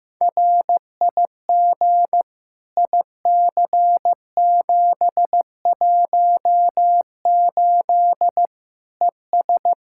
Morse code, heard as a keyed tone, about 15 wpm, 700 Hz. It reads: RIG IC718 ES
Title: Practice QSO - Exchange 2 - Info - 15wpm